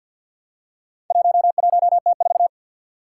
Morse code, heard as a keyed tone, about 51 wpm, 700 Hz. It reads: 11T4
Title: Four Challenging Letter-Number Combinations 50wpm